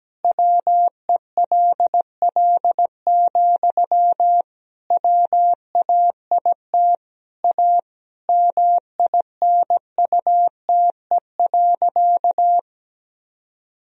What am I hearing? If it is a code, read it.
Morse code, heard as a keyed tone, about 17 wpm, 700 Hz. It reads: WELL, WAIT A MINUTE.
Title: Sentences from Top 500 Words and Abbreviations 17wpm